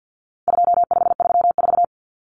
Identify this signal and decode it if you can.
Morse code, heard as a keyed tone, about 50 wpm, 700 Hz. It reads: 2534
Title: Numbers - 4-Digit 50wpm